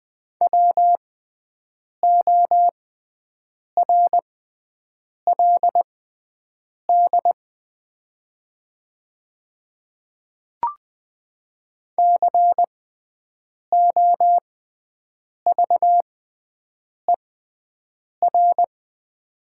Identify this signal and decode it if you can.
Morse code, heard as a keyed tone, about 20 wpm, spaced as for 7 wpm, 700 Hz. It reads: WORLD COVER
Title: 6x Character Spacing - Top 500 Words - 20wpm